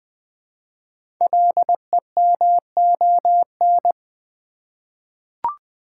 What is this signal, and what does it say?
Morse code, heard as a keyed tone, about 20 wpm, 700 Hz. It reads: LEMON
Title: 5 Letter Words - First 3 Letters Given - 20wpm